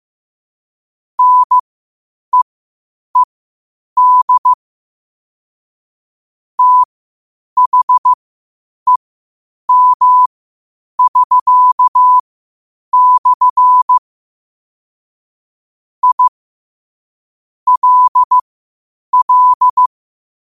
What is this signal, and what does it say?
Morse code, heard as a keyed tone, about 15 wpm, spaced as for 8 wpm, 1000 Hz. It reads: NEED THEM<SK>/ I LL
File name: qotdfiles / 8wpm